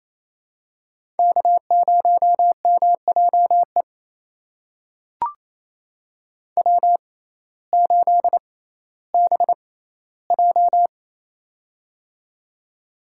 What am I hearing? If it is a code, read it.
Morse code, heard as a keyed tone, about 28 wpm, spaced as for 15 wpm, 700 Hz. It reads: K0MJE W8BJ
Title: USA Call Signs - 6x Character Spacing - 28wpm